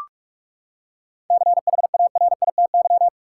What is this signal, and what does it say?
Morse code, heard as a keyed tone, about 46 wpm, 700 Hz. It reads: CHARITY
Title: Seven Letter Words - 45wpm